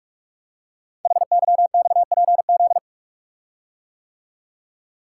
Morse code, heard as a keyed tone, about 44 wpm, 700 Hz. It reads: HYXPZ
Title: Code Groups - 5 Letters - 45wpm